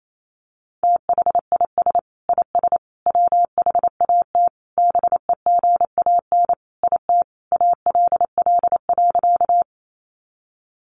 Morse code, heard as a keyed tone, about 28 wpm, 700 Hz. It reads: THIS IS WHAT BEGAN IT ALL.
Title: Sentences from Top 300 Words 28wpm